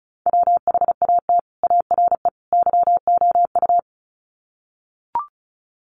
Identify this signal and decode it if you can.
Morse code, heard as a keyed tone, about 34 wpm, 700 Hz. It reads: WHAT ARE YOU
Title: Sets of 3 Words 35wpm